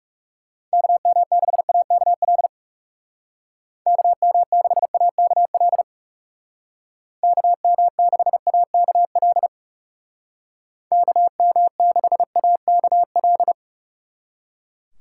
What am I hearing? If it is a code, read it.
Morse code, heard as a keyed tone, about 38 wpm, 700 Hz. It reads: KM6AKL KM6AKL KM6AKL KM6AKL
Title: Speed Racing - USA 2x3 Call Signs 30wpm